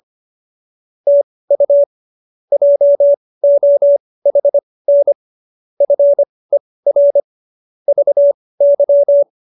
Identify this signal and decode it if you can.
Morse code, heard as a keyed tone, about 25 wpm, spaced as for 18 wpm, 570 Hz. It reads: TU JOHN FER VY